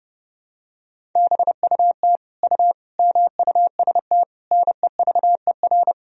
Morse code, heard as a keyed tone, about 29 wpm, 700 Hz. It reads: BUT U MUST NEVER
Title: Sentences from Top 300 Words and Abbreviations 30wpm